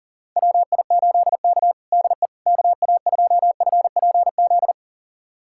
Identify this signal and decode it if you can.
Morse code, heard as a keyed tone, about 40 wpm, 700 Hz. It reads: WI8K DE KA2FPZ
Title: USA Callsign DE Callsign 40wpm